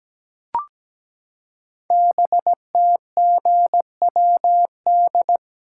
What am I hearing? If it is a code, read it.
Morse code, heard as a keyed tone, about 17 wpm, 700 Hz. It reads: BTGWD
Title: Code Groups - 5 Letters - 17wpm